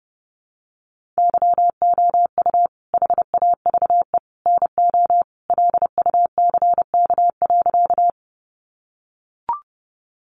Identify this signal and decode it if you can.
Morse code, heard as a keyed tone, about 30 wpm, 700 Hz. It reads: YOU HAVE NO LUCK.
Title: Sentences from Top 1500 Words 30wpm